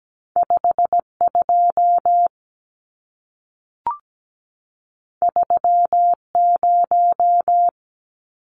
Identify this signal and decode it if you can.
Morse code, heard as a keyed tone, about 17 wpm, 700 Hz. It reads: 52 30
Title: Numbers - 2-Digit 17wpm